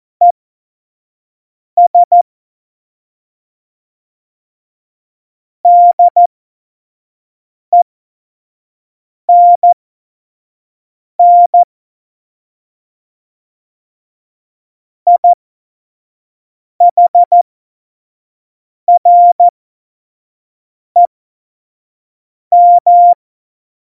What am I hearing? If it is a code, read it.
Morse code, heard as a keyed tone, about 14 wpm, spaced as for 5 wpm, 700 Hz. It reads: ES DENN IHREM
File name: Morsekurs / Witze / Witz 27 Juni 17, Tempo 25 F70